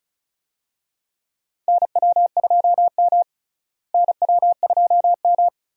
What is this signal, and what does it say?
Morse code, heard as a keyed tone, about 36 wpm, 700 Hz. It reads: NW2M NW2M
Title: USA Call Signs - 4 Characters - Repeated 3x - 35wpm